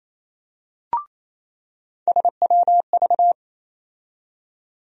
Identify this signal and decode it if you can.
Morse code, heard as a keyed tone, about 28 wpm, 700 Hz. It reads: SWV
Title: USA Call Sign Postfixes - 3 Letters 28wpm